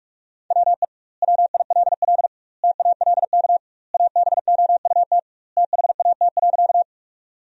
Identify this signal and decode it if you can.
Morse code, heard as a keyed tone, about 41 wpm, 700 Hz. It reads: WE WILL TALK ABOUT THAT.
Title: 2x Word-Spacing - Sentences from Top 400 Words - 45wpm